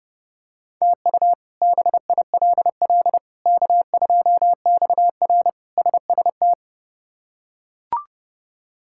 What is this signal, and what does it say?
Morse code, heard as a keyed tone, about 30 wpm, 700 Hz. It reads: TU BILL K2XR SST
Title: K1USN SST QSO Practice - 30wpm